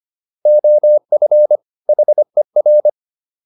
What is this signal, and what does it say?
Morse code, heard as a keyed tone, about 25 wpm, 600 Hz. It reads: OF HER